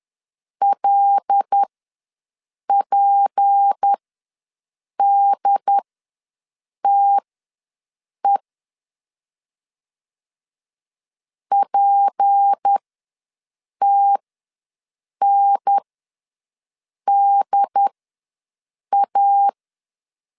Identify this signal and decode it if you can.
Morse code, heard as a keyed tone, about 10 wpm, spaced as for 6 wpm, 800 Hz. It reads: LPDTE PTNDA
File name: LEZIONI / 6 PKX0 / 6 P ws 10wpm 3751